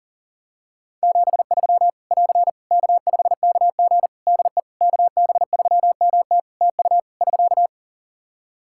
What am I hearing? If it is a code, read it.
Morse code, heard as a keyed tone, about 40 wpm, 700 Hz. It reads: 73 <AR> K5KG DE KB3MT TU <SK>
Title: K1USN SST QSO Practice 40wpm